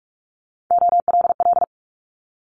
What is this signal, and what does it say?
Morse code, heard as a keyed tone, about 46 wpm, 700 Hz. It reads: OLL
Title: Multi-letter Phonemes Quick Fire 45wpm